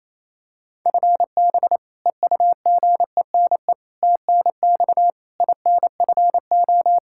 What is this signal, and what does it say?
Morse code, heard as a keyed tone, about 28 wpm, 700 Hz. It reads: FB EUGENE TNX INFO
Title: Practice QSO - Exchange 3 - More Info - 28wpm